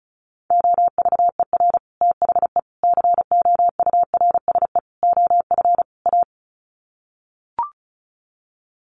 Morse code, heard as a keyed tone, about 34 wpm, 700 Hz. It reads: OVER THE COURSE OF A
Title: Sets of 5 Words Encore 35wpm